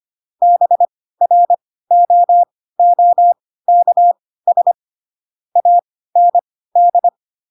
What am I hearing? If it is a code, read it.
Morse code, heard as a keyed tone, about 25 wpm, spaced as for 16 wpm, 700 Hz. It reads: BROOKS AND
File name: Pinocchio / Pinocchio ch4 10 25 ik1hge